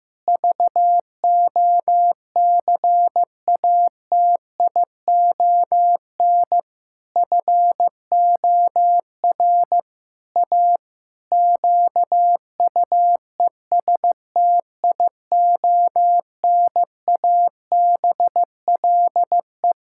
Morse code, heard as a keyed tone, about 15 wpm, 700 Hz. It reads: VOCATION FOR A QUESTIONABLE
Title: Pangram Challenge v2 - 15wpm